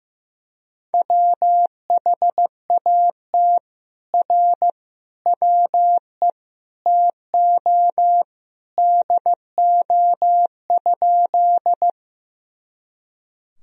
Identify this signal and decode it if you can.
Morse code, heard as a keyed tone, about 15 wpm, 700 Hz. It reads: WHAT R WE TO DO?
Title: Sentences from Top 100 Words and Abbreviations 15wpm